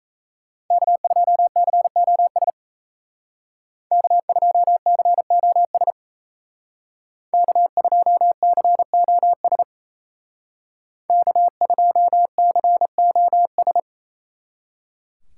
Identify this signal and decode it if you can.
Morse code, heard as a keyed tone, about 35 wpm, 700 Hz. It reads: K2COS K2COS K2COS K2COS